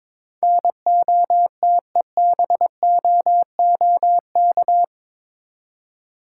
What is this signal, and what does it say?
Morse code, heard as a keyed tone, about 22 wpm, 700 Hz. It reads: NOTEBOOK
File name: practice / 22wpm / Eight Letter Words - 22wpm